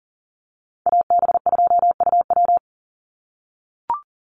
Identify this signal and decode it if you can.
Morse code, heard as a keyed tone, about 40 wpm, 700 Hz. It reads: AB2UW
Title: USA Call Signs 40wpm